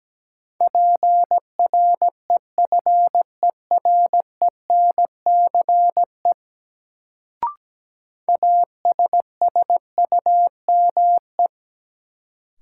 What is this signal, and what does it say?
Morse code, heard as a keyed tone, about 17 wpm, 700 Hz. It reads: PREFERENCE ASSUME
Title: Sentences from Top 1500 Words - New Word Drill - 17wpm